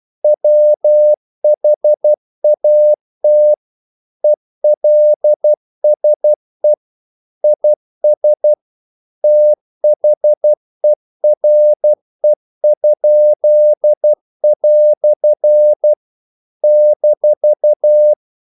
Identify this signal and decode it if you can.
Morse code, heard as a keyed tone, about 12 wpm, 600 Hz. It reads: WHAT ELSE IS THERE?" -
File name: CW / Practice / EnglishQuotations0112